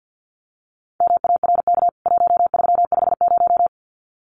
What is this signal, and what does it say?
Morse code, heard as a keyed tone, about 50 wpm, 700 Hz. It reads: MARK 1350
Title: CWT Exchange 50wpm